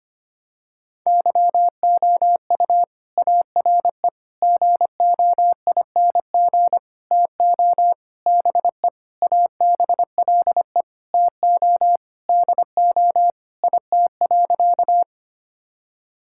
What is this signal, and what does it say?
Morse code, heard as a keyed tone, about 25 wpm, 700 Hz. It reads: YOU ARE GOING TO BE ABLE TO DO IT.
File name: practice / 25wpm / Sentences with 4-Word Phrases - 25wpm